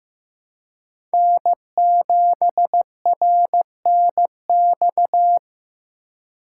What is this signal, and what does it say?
Morse code, heard as a keyed tone, about 16 wpm, 700 Hz. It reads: N7RNX
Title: USA Call Signs - 5 Characters - 3x Character Spacing - 15wpm